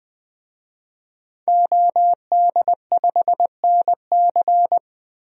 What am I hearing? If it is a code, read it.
Morse code, heard as a keyed tone, about 20 wpm, 700 Hz. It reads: OD5NC
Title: DXCC-100 Call Signs - 5 Characters - Repeated 2x - 20wpm